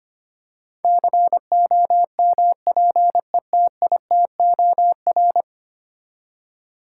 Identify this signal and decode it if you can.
Morse code, heard as a keyed tone, about 25 wpm, 700 Hz. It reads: COMPETITOR
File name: Ten Letter Words - 25wpm